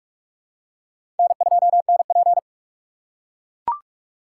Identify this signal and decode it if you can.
Morse code, heard as a keyed tone, about 42 wpm, 700 Hz. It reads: N2NP
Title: USA Call Signs - 4 Characters - 6x Character Spacing - 45wpm